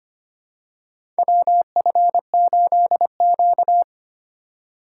Morse code, heard as a keyed tone, about 25 wpm, 700 Hz. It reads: WF8Q
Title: USA Call Signs - 4 Characters - 3x Character Spacing - 25wpm